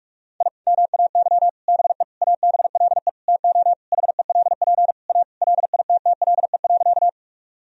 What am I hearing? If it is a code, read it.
Morse code, heard as a keyed tone, about 45 wpm, 700 Hz. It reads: I MAY BE ABLE TO HELP A LITTLE.
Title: Sentences from Top 1000 Words - Review Part 2 - 45wpm